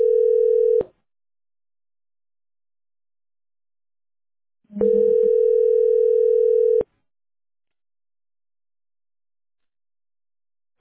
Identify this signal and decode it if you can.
Ringback tone (North America), 440 + 480 Hz.